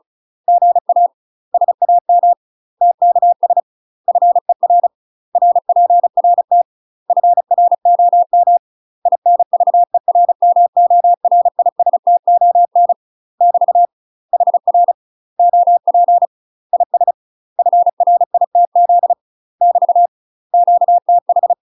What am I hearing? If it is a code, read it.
Morse code, heard as a keyed tone, about 36 wpm, 700 Hz. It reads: GA SAM TKS FER RPRT FROM INVERMORISTON <BT> HR OP IS FRITZ <BT> QTH
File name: UK MASTER QSOs TRAINING / 35WPM / UK MASTER QSOs 0040